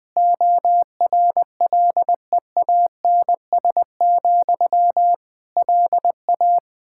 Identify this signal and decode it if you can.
Morse code, heard as a keyed tone, about 21 wpm, 700 Hz. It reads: ORLEANS, LA
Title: K1USN SST QSO Practice 20wpm